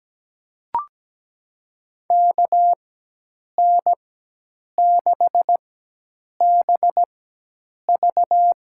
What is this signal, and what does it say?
Morse code, heard as a keyed tone, about 17 wpm, spaced as for 8 wpm, 700 Hz. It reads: KN6BV
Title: USA Call Signs - 4x Character Spacing - 17wpm